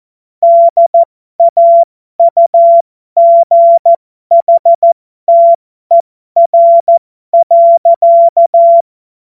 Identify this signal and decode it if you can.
Morse code, heard as a keyed tone, about 14 wpm, spaced as for 12 wpm, 680 Hz. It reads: DAUGHTER.
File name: Morsekurs / Witze